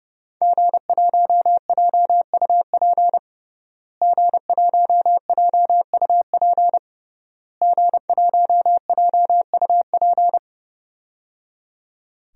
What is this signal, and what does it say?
Morse code, heard as a keyed tone, about 30 wpm, 700 Hz. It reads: G1JUP G1JUP G1JUP
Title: DXCC-100 Call Signs - 5 Characters - Repeated 3x - 30wpm